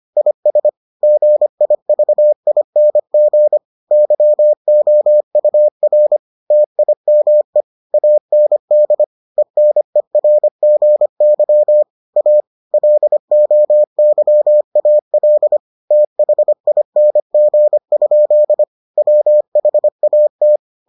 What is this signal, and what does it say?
Morse code, heard as a keyed tone, about 25 wpm, 600 Hz. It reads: IS GIVING YOUR TIME AND ENERGY A LOYAL THING? WHAT